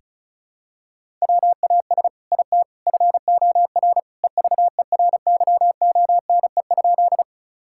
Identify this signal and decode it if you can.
Morse code, heard as a keyed tone, about 36 wpm, 700 Hz. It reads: WAS IT FOR EVERYONE?
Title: Sentences from Top 1000 Words - Review Part 5 - 35wpm